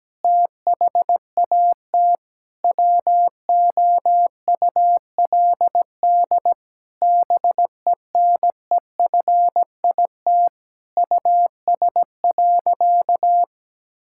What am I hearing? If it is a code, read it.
Morse code, heard as a keyed tone, about 17 wpm, 700 Hz. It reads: THAT WOULD BENEFIT US.
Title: Sentences from Top 1000 Words - Review Part 6 - 17wpm